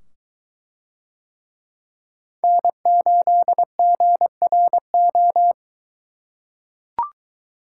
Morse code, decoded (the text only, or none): N8GRO